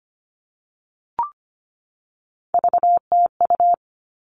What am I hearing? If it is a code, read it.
Morse code, heard as a keyed tone, about 25 wpm, 700 Hz. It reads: VTU
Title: All Cut Numbers - 3-Digit 25wpm